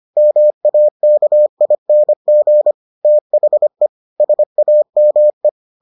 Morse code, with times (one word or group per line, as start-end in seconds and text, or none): 0.2-2.7 MAKING
3.0-3.9 THE
4.2-5.5 SAME